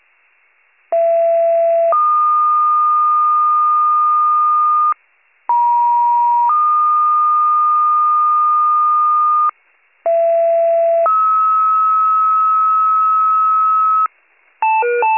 Two-tone page, heard as a one-second tone, then a three-second tone, about 670 then 1220 Hz, about 950 then 1220 Hz, about 670 then 1290 Hz. It is heard from 0.9 to 14.1 s.